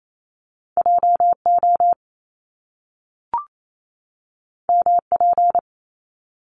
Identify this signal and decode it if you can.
Morse code, heard as a keyed tone, about 28 wpm, 700 Hz. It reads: JO MP
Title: Top Two Letter Combinations in Words 28wpm